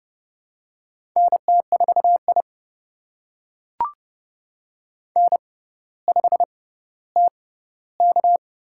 Morse code, heard as a keyed tone, about 32 wpm, spaced as for 16 wpm, 700 Hz. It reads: NT4I N5TK
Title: USA Call Signs - 4 Characters - 6x Character Spacing - 30wpm